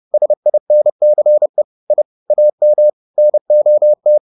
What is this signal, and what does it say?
Morse code, heard as a keyed tone, about 30 wpm, 600 Hz. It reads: SINCE I AM NOT
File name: the old man and the sea / the old man and the sea 1 30wpm